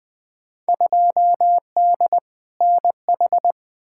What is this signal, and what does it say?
Morse code, heard as a keyed tone, about 20 wpm, 700 Hz. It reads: 2D NH
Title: Field Day Exchanges Encore - 20wpm